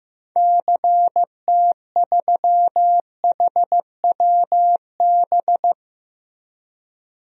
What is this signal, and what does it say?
Morse code, heard as a keyed tone, about 15 wpm, 700 Hz. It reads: CT3HWB
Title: DXCC-100 Call Signs - 6 Characters - Repeated 3x - 15wpm